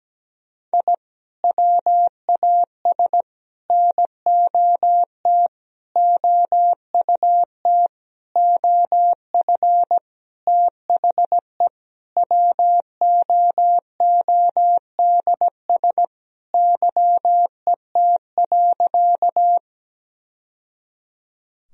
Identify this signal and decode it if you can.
Morse code, heard as a keyed tone, about 17 wpm, 700 Hz. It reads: I WAS NOT OUT OF THE WOODS YET.
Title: Sentences from Top 1000 Words - Review Part 2 - 17wpm